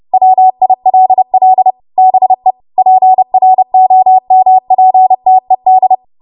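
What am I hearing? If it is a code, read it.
Morse code, heard as a keyed tone, about 30 wpm, 750 Hz. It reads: WILL BE PROMPTED
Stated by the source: Morse practice transmission at 30 WPM.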